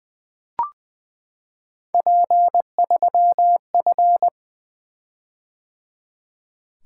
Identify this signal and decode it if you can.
Morse code, heard as a keyed tone, about 20 wpm, 700 Hz. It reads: P3F
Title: Three Letter-Number Combinations - 20wpm